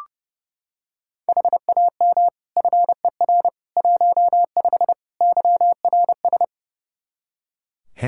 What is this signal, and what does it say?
Morse code, heard as a keyed tone, about 30 wpm, 700 Hz. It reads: HAM FER 15 YRS
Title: K1USN SST QSO Practice 30wpm